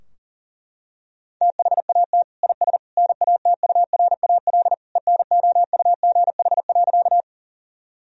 Morse code, heard as a keyed tone, about 40 wpm, 700 Hz. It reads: THAT IS NATURAL ENOUGH.
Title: Sentences from Top 1000 Words - Review Part 4 - 40wpm